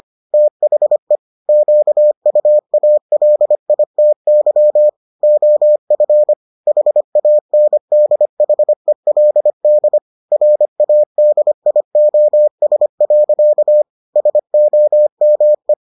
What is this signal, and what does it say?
Morse code, heard as a keyed tone, about 25 wpm, 600 Hz. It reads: THE QUALITY OF HANDHELD RADIOS. SOME